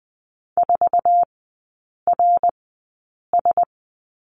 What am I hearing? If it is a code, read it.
Morse code, heard as a keyed tone, about 20 wpm, spaced as for 8 wpm, 700 Hz. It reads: 4RS
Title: 053 Lesson 10e - Letter r - Three Characters 20wpm